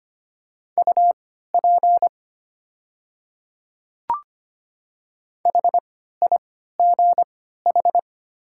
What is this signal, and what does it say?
Morse code, heard as a keyed tone, about 25 wpm, spaced as for 14 wpm, 700 Hz. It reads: UP HIGH
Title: 3x Character Spacing - Top 200 Words - 25wpm